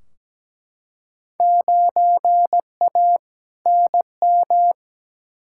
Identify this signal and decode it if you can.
Morse code, heard as a keyed tone, about 17 wpm, 700 Hz. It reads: 9A NM
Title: Field Day Exchanges Encore - 17wpm